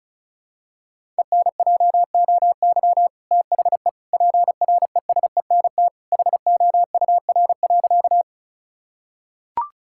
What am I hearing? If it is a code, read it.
Morse code, heard as a keyed tone, about 36 wpm, 700 Hz. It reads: ENJOY THE PRESENT HOUR.
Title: Sentences from Top 1000 Words - Review Part 5 - 35wpm